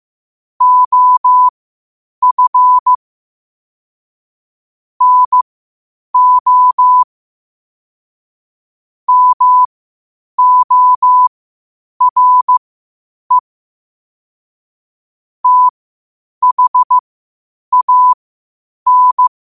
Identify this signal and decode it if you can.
Morse code, heard as a keyed tone, about 15 wpm, spaced as for 8 wpm, 1000 Hz. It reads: OF NO MORE THAN
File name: qotdfiles / 8wpm